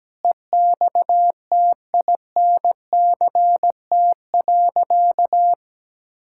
Morse code, heard as a keyed tone, about 17 wpm, 700 Hz. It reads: EXTINCT.
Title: Pangram Challenge v2 - 17wpm